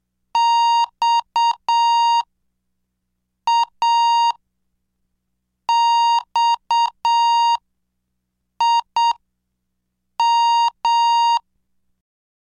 Morse code, heard as a keyed tone, about 7 wpm, spaced as for 5 wpm, 930 Hz. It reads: XAXIM
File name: cursos / CDestudo / 17 Track 17